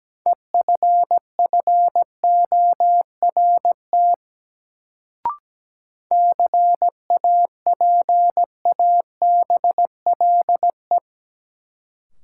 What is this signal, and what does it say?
Morse code, heard as a keyed tone, about 17 wpm, 700 Hz. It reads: EFFORT CAPABLE
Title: Sentences from Top 1200 Words - New Word Drill - 17wpm